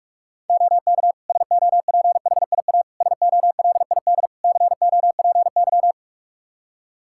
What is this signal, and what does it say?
Morse code, heard as a keyed tone, about 46 wpm, 700 Hz. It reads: OK SOPHIA SOLID COPY
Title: Practice QSO - Exchange 3 - More Info - 45wpm